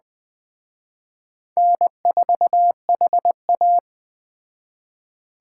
Morse code, heard as a keyed tone, about 20 wpm, 700 Hz. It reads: N4HA
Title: Speed Racing - USA Call Signs 17wpm